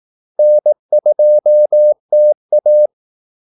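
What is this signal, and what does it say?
Morse code, heard as a keyed tone, about 18 wpm, 600 Hz. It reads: N2TA
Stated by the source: Morse call-sign practice at 18 wpm.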